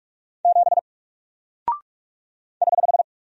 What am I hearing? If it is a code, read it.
Morse code, heard as a keyed tone, about 45 wpm, 700 Hz. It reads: 7 <HH>
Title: Single Character (Letter:Number:Punctuation:Prosign) Alt 45wpm